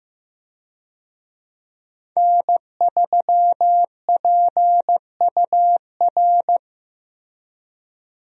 Morse code, heard as a keyed tone, about 15 wpm, 700 Hz. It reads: N3PUR
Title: USA Call Signs - 5 Characters - Repeated 3x - 15wpm